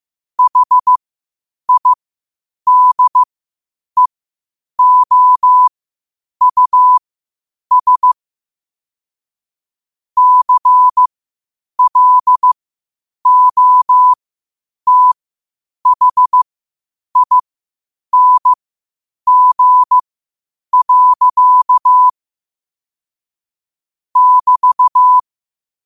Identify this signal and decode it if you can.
Morse code, heard as a keyed tone, about 15 wpm, spaced as for 8 wpm, 1000 Hz. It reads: HIDEOUS CLOTHING. <BT>
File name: qotdfiles / 8wpm